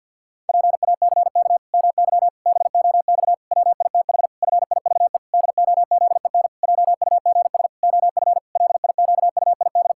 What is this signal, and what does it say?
Morse code, heard as a keyed tone, about 51 wpm, 700 Hz. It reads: PACK MY BOX WITH FIVE DOZEN JUGS OF LIQUID
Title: Pangram Challenge v2 - 50wpm